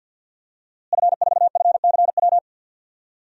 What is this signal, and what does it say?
Morse code, heard as a keyed tone, about 52 wpm, 700 Hz. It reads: F4FCW
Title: DXCC-100 Call Signs - 5 Characters - Repeated 3x - 50wpm